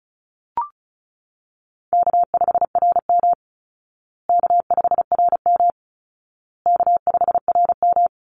Morse code, heard as a keyed tone, about 34 wpm, 700 Hz. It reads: K5RM K5RM K5RM
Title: Easy Call Signs Repeated 3x - 35wpm